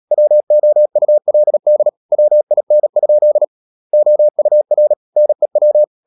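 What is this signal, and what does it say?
Morse code, heard as a keyed tone, about 37 wpm, 600 Hz. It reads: WOULD WIN? OUR NEW
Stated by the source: Morse code practice at 35–39 wpm sending English text of Ken Follett's novel Never, from CW med Gnister.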